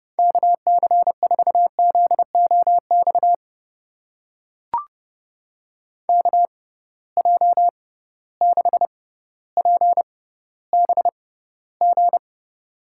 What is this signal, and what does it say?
Morse code, heard as a keyed tone, about 31 wpm, spaced as for 16 wpm, 700 Hz. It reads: KC4ZOX KJ6PBG
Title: USA Call Signs - 6 Characters - 6x Character Spacing - 30wpm